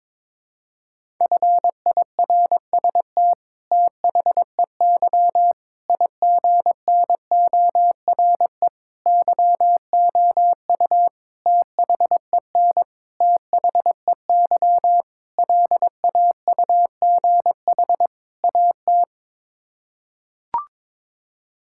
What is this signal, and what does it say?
Morse code, heard as a keyed tone, about 22 wpm, 700 Hz. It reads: FIRST THEY IGNORE YOU THEN THEY LAUGH AT
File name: practice / 22wpm / Sets of 8 Words - 22wpm